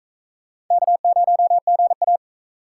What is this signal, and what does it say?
Morse code, heard as a keyed tone, about 42 wpm, 700 Hz. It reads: K0GA